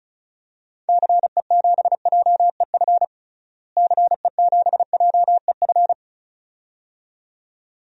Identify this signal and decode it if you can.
Morse code, heard as a keyed tone, about 34 wpm, 700 Hz. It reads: CE7JEF CE7JEF
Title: DXCC-100 Call Signs - 6 Characters - Repeated 2x - 35wpm